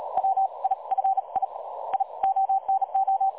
Morse code, heard as a keyed tone, about 36 wpm, spaced as for 26 wpm, 770 Hz. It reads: WI FII ONO
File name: dlazaawansowanych / nagraniazpasma / websdr recording 3565.0kHz